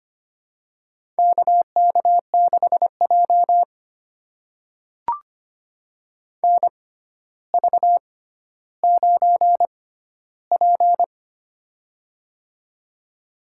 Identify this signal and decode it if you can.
Morse code, heard as a keyed tone, about 25 wpm, spaced as for 13 wpm, 700 Hz. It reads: KK6J NV9P
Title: USA Call Signs - 6x Character Spacing - 25wpm